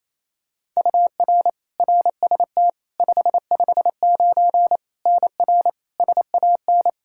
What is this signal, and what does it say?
Morse code, heard as a keyed tone, about 28 wpm, 700 Hz. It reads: UR RST 559 NR SAN
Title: K1USN SST QSO Practice 28wpm